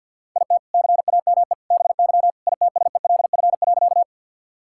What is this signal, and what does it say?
Morse code, heard as a keyed tone, about 50 wpm, 700 Hz. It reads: IT CAME BY ITSELF.
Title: Sentences from Top 1000 Words - Review Part 5 - 50wpm